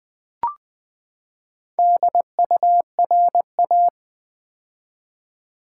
Morse code, heard as a keyed tone, about 20 wpm, 700 Hz. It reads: DURA
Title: Build A Word - 4 Letters At A Time - 8-Letter Words - 20wpm